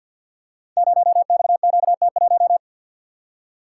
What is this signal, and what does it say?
Morse code, heard as a keyed tone, about 50 wpm, 700 Hz. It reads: 0XQT1
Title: Code Groups - 5 Letter:Numbers - 50wpm